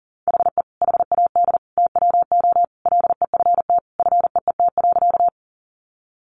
Morse code, heard as a keyed tone, about 40 wpm, 700 Hz. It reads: HE HAD TWO LEFT FEET.
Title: Sentences from Top 300 Words 40wpm